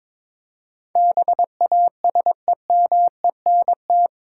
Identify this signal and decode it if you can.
Morse code, heard as a keyed tone, about 22 wpm, 700 Hz. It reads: BASEMENT
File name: practice / 22wpm / Eight Letter Words - 22wpm